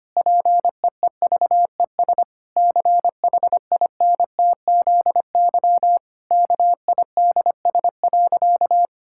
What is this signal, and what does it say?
Morse code, heard as a keyed tone, about 25 wpm, 700 Hz. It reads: PEEVES CHINTZY KIDS.
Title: Pangram Challenge v2 - 25wpm